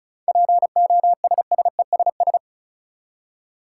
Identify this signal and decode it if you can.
Morse code, heard as a keyed tone, about 36 wpm, 700 Hz. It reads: POSSESS
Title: Seven Letter Words - 35wpm